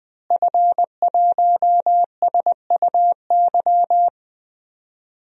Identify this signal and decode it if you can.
Morse code, heard as a keyed tone, about 21 wpm, 700 Hz. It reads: F1SUY
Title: DXCC-100 Call Signs - 5 Characters - Repeated 2x - 20wpm